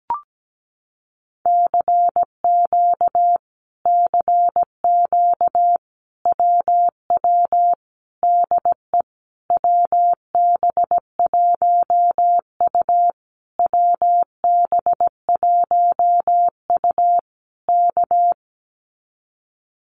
Morse code, heard as a keyed tone, about 17 wpm, 700 Hz. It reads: CQ CQ WW DE WB1U WB1U K
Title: Common QSO Phrases Encore 17wpm